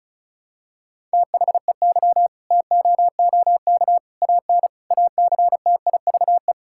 Morse code, heard as a keyed tone, about 36 wpm, 700 Hz. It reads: THEY TOOK AN ACTIVE
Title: Sentences from Top 1000 Words - Review Part 6 - 35wpm